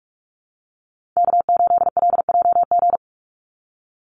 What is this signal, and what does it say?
Morse code, heard as a keyed tone, about 44 wpm, 700 Hz. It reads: K8LJG
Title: NA SOTA Chaser Callsigns 45wpm